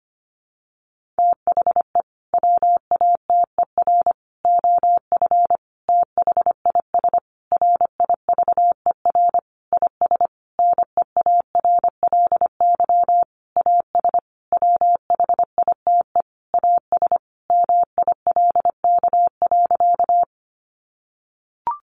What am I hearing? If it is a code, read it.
Morse code, heard as a keyed tone, about 25 wpm, 700 Hz. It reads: THE WATER OF THIS RIVER IS NEARLY AS WHITE AS MILK.
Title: Sentences from Top 1500 Words 25wpm